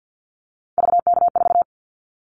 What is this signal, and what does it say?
Morse code, heard as a keyed tone, about 50 wpm, 700 Hz. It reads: VK4
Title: Callsign Prefixes for Top 50 DXCC - 50wpm